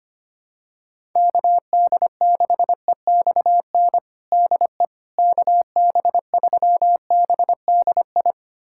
Morse code, heard as a keyed tone, about 25 wpm, 700 Hz. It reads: KD6EXN DE KB3BDI
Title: USA Callsign DE Callsign 25wpm